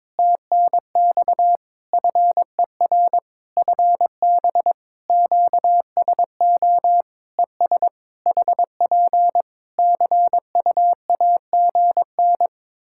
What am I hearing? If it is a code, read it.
Morse code, heard as a keyed tone, about 22 wpm, 700 Hz. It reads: TNX FER FB QSO ES HP CUAGN
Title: Practice QSO - Exchange 4 - Ending - 22wpm